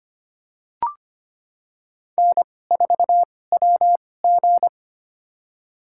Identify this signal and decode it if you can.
Morse code, heard as a keyed tone, about 25 wpm, spaced as for 18 wpm, 700 Hz. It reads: N4WG
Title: USA Call Signs - 2x Character Spacing - 25wpm